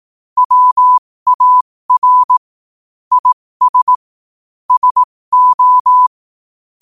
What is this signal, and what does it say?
Morse code, heard as a keyed tone, about 18 wpm, spaced as for 15 wpm, 1000 Hz. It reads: WAR IS SO
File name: qotdfiles / 15wpm